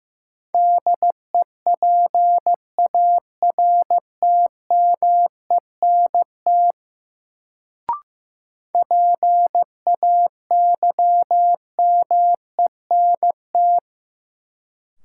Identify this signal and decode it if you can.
Morse code, heard as a keyed tone, about 15 wpm, 700 Hz. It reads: DEPARTMENT PAYMENT
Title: Sentences from Top 1200 Words - New Word Drill Alt - 15wpm